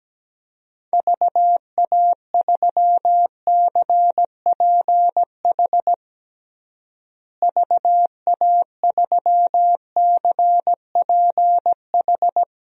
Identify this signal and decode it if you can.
Morse code, heard as a keyed tone, about 17 wpm, 700 Hz. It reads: VA3CPH VA3CPH
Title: DXCC-100 Call Signs - 6 Characters - Repeated 3x - 17wpm